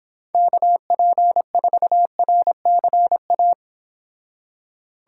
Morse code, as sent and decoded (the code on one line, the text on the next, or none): -.- .--. ....- .-. -.-. .-
KP4RCA